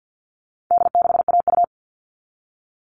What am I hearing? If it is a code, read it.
Morse code, heard as a keyed tone, about 50 wpm, 700 Hz. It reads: D6AU
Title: All Cut Numbers - 4-Digit 50wpm